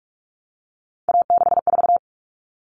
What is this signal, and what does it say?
Morse code, heard as a keyed tone, about 44 wpm, 700 Hz. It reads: A64
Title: Common Cut Numbers - 3-Digit 45wpm